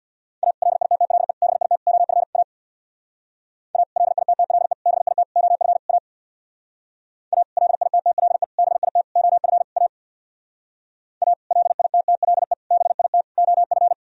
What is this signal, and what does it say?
Morse code, heard as a keyed tone, about 63 wpm, 700 Hz. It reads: A LITTLE BIT OF A A LITTLE BIT OF A A LITTLE BIT OF A A LITTLE BIT OF